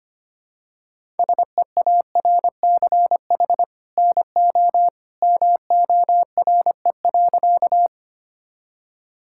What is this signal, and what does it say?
Morse code, heard as a keyed tone, about 25 wpm, 700 Hz. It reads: SEARCH NO MORE.
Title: Sentences from Top 1000 Words - Review Part 5 - 25wpm